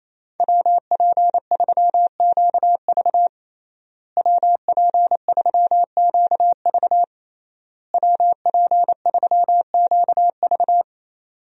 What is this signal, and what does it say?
Morse code, heard as a keyed tone, about 28 wpm, 700 Hz. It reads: WP3QV WP3QV WP3QV
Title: USA Call Signs - 5 Characters - Repeated 3x - 28wpm